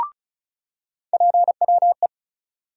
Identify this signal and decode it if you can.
Morse code, heard as a keyed tone, about 34 wpm, 700 Hz. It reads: PWE
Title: USA Call Sign Postfixes - 3 Letters 35wpm